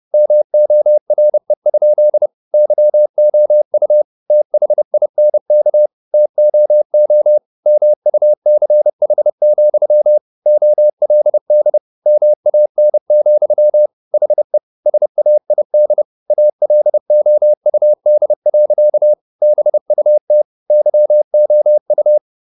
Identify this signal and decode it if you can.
Morse code, heard as a keyed tone, about 30 wpm, 600 Hz. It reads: MORE? YOU THINK TOO MUCH, OLD MAN, HE SAID ALOUD. BUT YOU